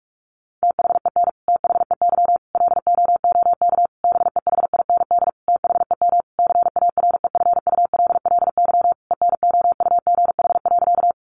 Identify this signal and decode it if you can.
Morse code, heard as a keyed tone, about 46 wpm, 700 Hz. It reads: THEN THEY LOOK BEHIND THEM CAREFULLY ENOUGH.
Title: Sentences from Top 1300 Words 45wpm